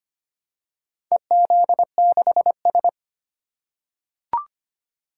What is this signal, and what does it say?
Morse code, heard as a keyed tone, about 25 wpm, 700 Hz. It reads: EZ6S
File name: practice / 25wpm / Four Letter-Number Combinations - 25wpm